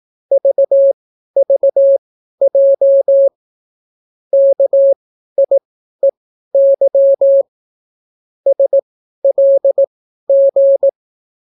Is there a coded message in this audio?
Yes: Morse code from 0.3 to 10.9 s, heard as a keyed tone, about 18 wpm, spaced as for 12 wpm, 550 Hz. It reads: VVJ KIEY SLG